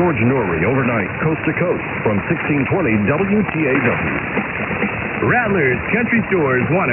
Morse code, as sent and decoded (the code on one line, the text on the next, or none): none